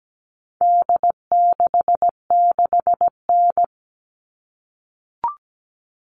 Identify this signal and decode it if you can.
Morse code, heard as a keyed tone, about 17 wpm, 700 Hz. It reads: D66N
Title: All Cut Numbers - 4-Digit 17wpm